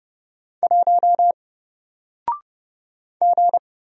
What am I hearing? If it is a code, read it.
Morse code, heard as a keyed tone, about 30 wpm, 700 Hz. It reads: 1 G
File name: Single-Letter-Number-30wpm